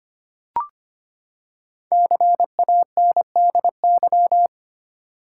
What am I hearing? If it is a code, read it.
Morse code, heard as a keyed tone, about 25 wpm, 700 Hz. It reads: CANDY
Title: Five Letter Words - 25wpm